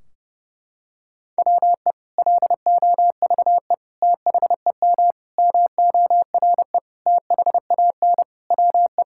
Morse code, heard as a keyed tone, about 30 wpm, 700 Hz. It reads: WE LOVE THEM MORE THAN WE
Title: Sentences from Top 1600 Words 30wpm